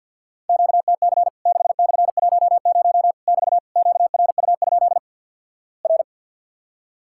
Morse code, heard as a keyed tone, about 48 wpm, 700 Hz. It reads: QTC 6/10 <BT> QRU? R
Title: Worked All Europe DX Contest - QTC Exchange - 50wpm